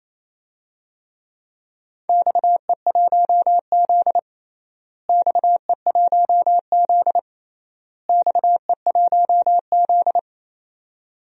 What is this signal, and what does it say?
Morse code, heard as a keyed tone, about 29 wpm, 700 Hz. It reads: XE1Z XE1Z XE1Z
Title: DXCC-100 Call Signs - 4 Characters - Repeated 3x - 28wpm